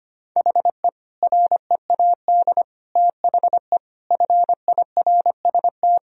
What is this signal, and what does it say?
Morse code, heard as a keyed tone, about 25 wpm, 700 Hz. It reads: HE READ THE FIRST
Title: Sentences from Top 1000 Words - Review Part 1 - 25wpm